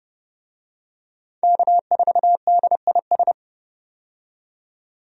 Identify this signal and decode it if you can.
Morse code, heard as a keyed tone, about 30 wpm, 700 Hz. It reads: K4DIS
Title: USA Call Signs - 5 Characters - 30wpm